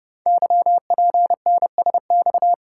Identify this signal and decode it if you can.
Morse code, heard as a keyed tone, about 30 wpm, 700 Hz. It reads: YPNSX
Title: Code Groups - 5 Letter:Numbers - 30wpm